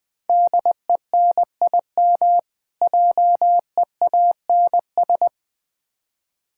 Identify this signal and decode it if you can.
Morse code, heard as a keyed tone, about 21 wpm, 690 Hz. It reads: DENIM JEANS
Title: Difficult Words with Hints - 20wpm